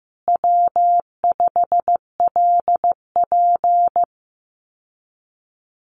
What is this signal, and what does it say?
Morse code, heard as a keyed tone, about 14 wpm, 700 Hz. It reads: W5LP
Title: Common QSO Phrases Encore 15wpm